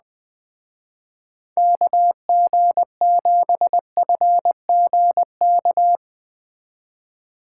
Morse code, heard as a keyed tone, about 21 wpm, 700 Hz. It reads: KG7FGK